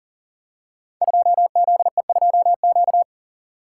Morse code, heard as a keyed tone, about 41 wpm, 700 Hz. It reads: 2ZE2Q
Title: Code Groups - 5 Letter:Numbers - 40wpm